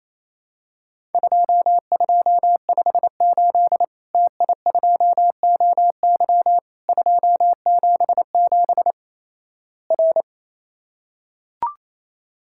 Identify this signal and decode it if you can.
Morse code, heard as a keyed tone, about 28 wpm, 700 Hz. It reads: 2258 TI2OY 277 R
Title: Worked All Europe DX Contest - QTC Exchange - 28wpm